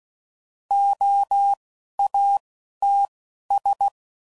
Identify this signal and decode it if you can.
Morse code, heard as a keyed tone, about 16 wpm, spaced as for 11 wpm, 800 Hz. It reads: OATS
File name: Prefixes / By First Letter / Os